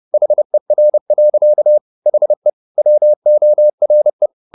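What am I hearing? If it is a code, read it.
Morse code, heard as a keyed tone, about 30 wpm, 600 Hz. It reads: HER. HE WORE
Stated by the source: Morse code at 30 wpm, the novel Never by Ken Follett in English, from CW med Gnister.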